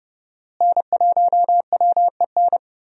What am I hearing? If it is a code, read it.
Morse code, heard as a keyed tone, about 30 wpm, 700 Hz. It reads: N1WEN
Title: USA Call Signs - 5 Characters - Repeated 2x - 30wpm